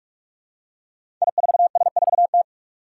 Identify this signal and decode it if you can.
Morse code, heard as a keyed tone, about 44 wpm, 700 Hz. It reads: I4S4T
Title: Five Challenging Letter-Number Combinations 45wpm